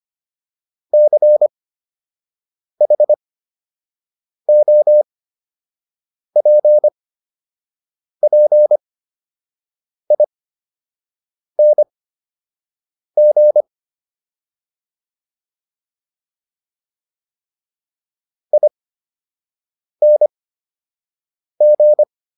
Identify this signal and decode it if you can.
Morse code, heard as a keyed tone, about 25 wpm, spaced as for 6 wpm, 600 Hz. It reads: CHOPPING ING